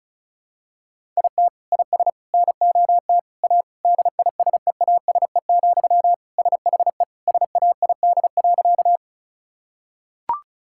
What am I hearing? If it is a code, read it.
Morse code, heard as a keyed tone, about 34 wpm, 700 Hz. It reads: IT IS NOT A DISEASE, SHE SAID.
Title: Sentences from Top 1000 Words - Review Part 6 - 35wpm